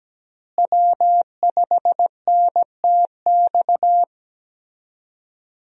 Morse code, heard as a keyed tone, about 17 wpm, 700 Hz. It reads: W5NTX
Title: USA Call Signs - 5 Characters - Repeated 2x - 17wpm